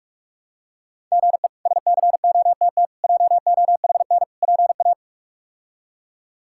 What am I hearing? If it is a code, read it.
Morse code, heard as a keyed tone, about 44 wpm, 700 Hz. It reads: GE SCOTT JOHN PA
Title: K1USN SST QSO Practice - 45wpm